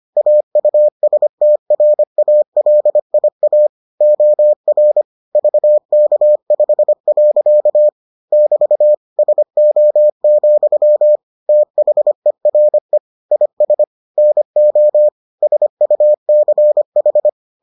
Morse code, heard as a keyed tone, about 25 wpm, 600 Hz. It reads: AUSTRALIA OR VK5. <BT> SO, THERE IS NO SUCH